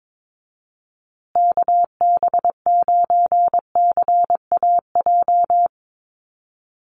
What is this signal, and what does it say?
Morse code, heard as a keyed tone, about 22 wpm, 700 Hz. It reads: KB9CAJ
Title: USA Call Signs Repeated 2x 22wpm